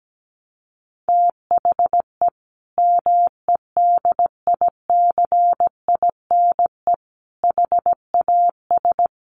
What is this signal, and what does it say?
Morse code, heard as a keyed tone, about 17 wpm, 700 Hz. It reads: THE MEDICINE HAS
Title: Sentences from Top 900 Words 17wpm